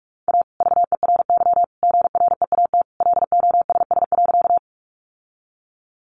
Morse code, heard as a keyed tone, about 44 wpm, 700 Hz. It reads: A VERY GREAT LOSS.
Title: Sentences from Top 1400 Words 45wpm